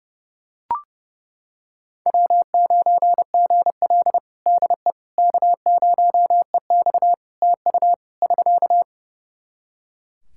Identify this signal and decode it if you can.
Morse code, heard as a keyed tone, about 30 wpm, 700 Hz. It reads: W9GL DE K0EX TU <SK>
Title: Practice QSO - Exchange 4 - Ending - 30wpm